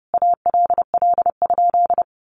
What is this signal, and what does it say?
Morse code, heard as a keyed tone, about 30 wpm, 700 Hz. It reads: ALL?
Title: Sentences from Top 200 Words 30wpm